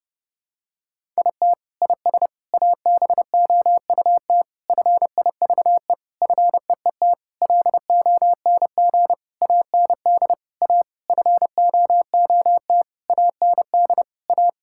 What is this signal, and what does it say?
Morse code, heard as a keyed tone, about 30 wpm, 700 Hz. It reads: IT IS ABOUT FIVE FEET LONG AND A FOOT AND A
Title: Sentences from Top 1000 Words - Review Part 6 - 30wpm